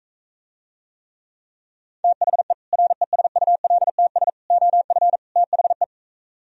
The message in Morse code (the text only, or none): THE RESULTS OF THE